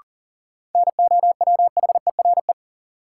Morse code, heard as a keyed tone, about 40 wpm, 700 Hz. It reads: NOWHERE
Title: Sets of 5 Words Encore - New Word Drill - 40wpm